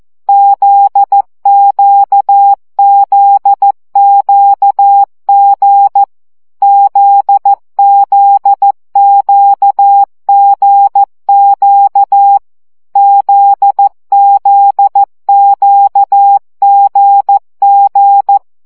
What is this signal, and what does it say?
Morse code, heard as a keyed tone, about 14 wpm, 800 Hz. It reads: ZQZQG ZZQGQ ZZQGG
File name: CW gyakorlas g z q 12 baud